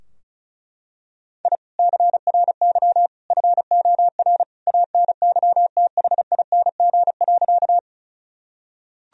Morse code, heard as a keyed tone, about 35 wpm, 700 Hz. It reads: I CRY FOR ANYTHING.
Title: Sentences from Top 1000 Words - Review Part 2 - 35wpm